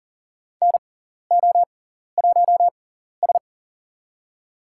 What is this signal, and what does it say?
Morse code, heard as a keyed tone, about 40 wpm, spaced as for 14 wpm, 700 Hz. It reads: NO1S
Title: USA Call Signs - 6x Character Spacing - 40wpm